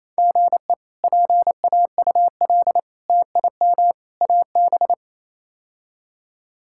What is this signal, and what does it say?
Morse code, heard as a keyed tone, about 27 wpm, 700 Hz. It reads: GE PAUL TIM AB
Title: K1USN SST QSO Practice - 28wpm